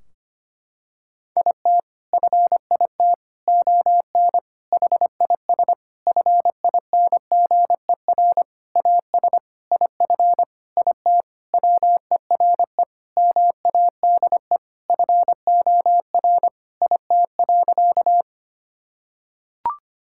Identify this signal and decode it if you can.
Morse code, heard as a keyed tone, about 25 wpm, 700 Hz. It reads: IT FIT ON HIS FINGER AS IF IT WERE MADE FOR IT.
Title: Sentences from Top 1600 Words 25wpm